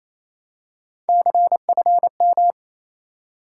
Morse code, heard as a keyed tone, about 28 wpm, 700 Hz. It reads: CFM
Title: Top 90 QSO Elements 28wpm